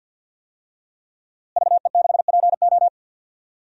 Morse code, heard as a keyed tone, about 52 wpm, 700 Hz. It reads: VE6PO
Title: DXCC-100 Call Signs - 5 Characters - Repeated 3x - 50wpm